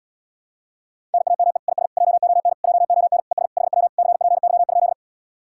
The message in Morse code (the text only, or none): NAME IS JON JON ES HW CPY?